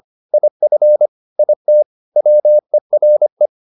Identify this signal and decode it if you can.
Morse code, heard as a keyed tone, about 25 wpm, 600 Hz. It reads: IF IT WERE
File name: the old man and the sea part 1 25wpm